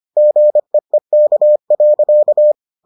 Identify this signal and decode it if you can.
Morse code, heard as a keyed tone, about 25 wpm, 600 Hz. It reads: GEEK.